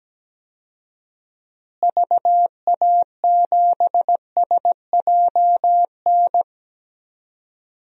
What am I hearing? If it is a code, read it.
Morse code, heard as a keyed tone, about 17 wpm, 700 Hz. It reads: VA7SJN
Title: DXCC-100 Call Signs - 6 Characters - Repeated 3x - 17wpm